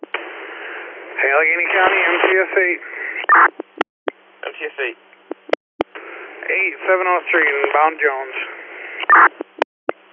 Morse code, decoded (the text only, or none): none